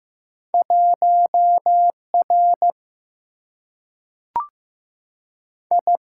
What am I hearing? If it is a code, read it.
Morse code, heard as a keyed tone, about 15 wpm, 700 Hz. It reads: 1 R I
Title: Single-Letter Single-Number 15wpm